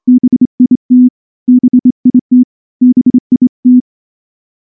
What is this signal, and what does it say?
Morse code, heard as a keyed tone, about 25 wpm, 250 Hz. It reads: BIT BIT BIT
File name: morse / morse-training-25WPM